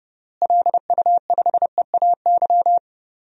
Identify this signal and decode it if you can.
Morse code, heard as a keyed tone, about 30 wpm, 700 Hz. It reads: LU5EAY
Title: DXCC-100 Call Signs - 6 Characters - Repeated 3x - 30wpm